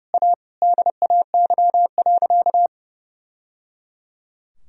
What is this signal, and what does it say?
Morse code, heard as a keyed tone, about 30 wpm, 700 Hz. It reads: A DAY.
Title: Sentences from Top 1000 Words - Review Part 3 - 30wpm